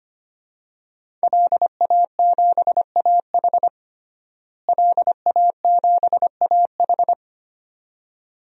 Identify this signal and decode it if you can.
Morse code, heard as a keyed tone, about 25 wpm, 700 Hz. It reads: LA7AH LA7AH
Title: DXCC-100 Call Signs - 5 Characters - Repeated 2x - 25wpm